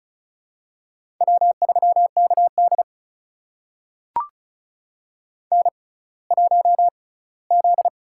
Morse code, heard as a keyed tone, about 36 wpm, spaced as for 20 wpm, 700 Hz. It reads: W3KD N1Z
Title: USA Call Signs - 4 Characters - 6x Character Spacing - 35wpm